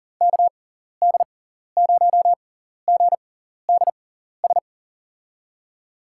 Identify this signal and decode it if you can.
Morse code, heard as a keyed tone, about 40 wpm, spaced as for 14 wpm, 700 Hz. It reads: KD0GDS
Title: USA Call Signs - 6 Characters - 6x Character Spacing - 40wpm